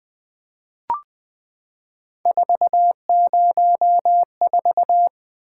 Morse code, heard as a keyed tone, about 20 wpm, 700 Hz. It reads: 404
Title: Three Letter-Number Combinations - 20wpm